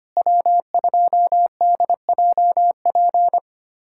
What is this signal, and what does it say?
Morse code, heard as a keyed tone, about 25 wpm, 700 Hz. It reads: W2DJP
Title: USA Callsign DE Callsign - 5 Characters - 25wpm